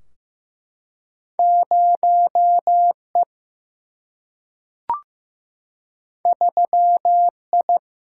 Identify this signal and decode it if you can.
Morse code, heard as a keyed tone, about 16 wpm, 700 Hz. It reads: 0E 3I
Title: Single-Letter Single-Number 15wpm